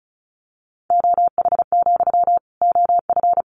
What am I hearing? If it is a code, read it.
Morse code, heard as a keyed tone, about 34 wpm, 700 Hz. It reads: OH, OF
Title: Sentences from Top 500 Words 35wpm